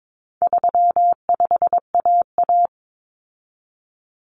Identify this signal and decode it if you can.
Morse code, heard as a keyed tone, about 22 wpm, 700 Hz. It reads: 35AA
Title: Common Cut Numbers - 4-Digit 22wpm